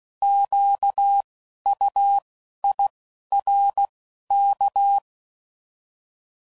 Morse code, heard as a keyed tone, about 16 wpm, 800 Hz. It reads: Q U I R K
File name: Prefixes / By First Letter / Qs